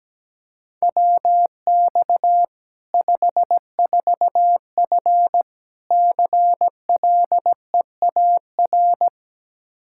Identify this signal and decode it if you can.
Morse code, heard as a keyed tone, about 17 wpm, 700 Hz. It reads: WX 54F CLEAR
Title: K1USN SST QSO Practice 17wpm